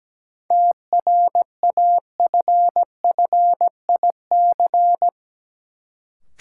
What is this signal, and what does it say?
Morse code, heard as a keyed tone, about 17 wpm, 700 Hz. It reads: TRAFFIC
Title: Sentences from Top 1300 Words - New Word Drill Alt - 17wpm